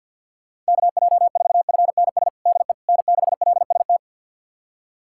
Morse code, heard as a keyed tone, about 51 wpm, 700 Hz. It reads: KJ4FNS DE N6LST
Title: USA Callsign DE Callsign 50wpm